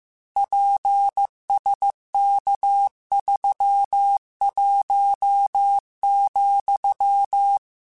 Morse code, heard as a keyed tone, about 15 wpm, 800 Hz. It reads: PSK31,
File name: QSOs / QSOs-mp3s / WA7QOX 2@13